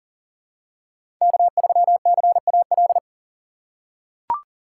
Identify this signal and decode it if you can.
Morse code, heard as a keyed tone, about 40 wpm, 700 Hz. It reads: K3CAL
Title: USA Call Signs - 5 Characters - 3x Character Spacing - 40wpm